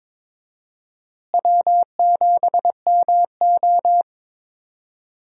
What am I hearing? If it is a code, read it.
Morse code, heard as a keyed tone, about 22 wpm, 700 Hz. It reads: W7MO
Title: USA Call Signs - 4 Characters - Repeated 3x - 22wpm